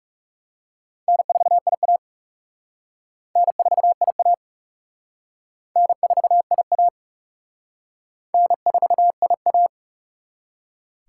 Morse code, heard as a keyed tone, about 36 wpm, 700 Hz. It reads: N4IA N4IA N4IA N4IA
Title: Speed Racing - USA Call Signs 30wpm